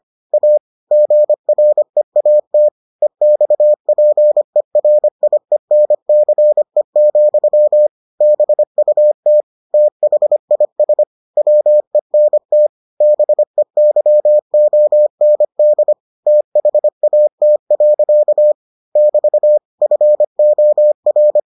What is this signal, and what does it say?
Morse code, heard as a keyed tone, about 25 wpm, 600 Hz. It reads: A GREAT EXPERIENCE, BUT THIS WENT BEYOND THAT. <BT> FOR